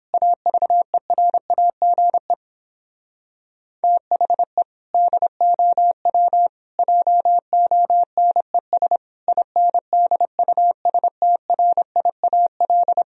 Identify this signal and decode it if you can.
Morse code, heard as a keyed tone, about 26 wpm, 700 Hz. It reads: AVERAGE THE DOW JONES INDUSTRIAL
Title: Speed-Racing - Sets of 5 Words 22wpm